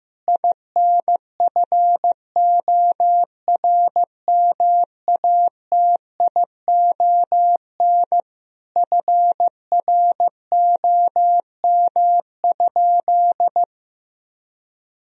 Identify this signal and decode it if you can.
Morse code, heard as a keyed tone, about 15 wpm, 700 Hz. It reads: INFORMATION FROM?
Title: Sentences from Top 1000 Words - Review Part 3 - 15wpm